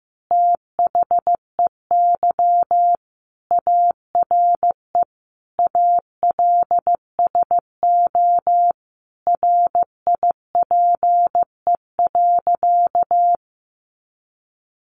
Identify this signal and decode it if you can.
Morse code, heard as a keyed tone, about 15 wpm, 700 Hz. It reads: THEY ARE ALSO RIPE.
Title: Additional Easy Phrases 15wpm